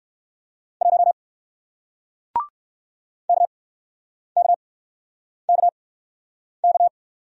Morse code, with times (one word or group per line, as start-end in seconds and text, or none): 0.8-1.1 1
3.3-6.9 KKKK